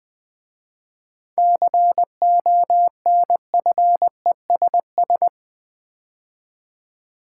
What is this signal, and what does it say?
Morse code, heard as a keyed tone, about 20 wpm, 700 Hz. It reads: CONFESS
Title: 7 Letter Words - First 3 Letters Given - 20wpm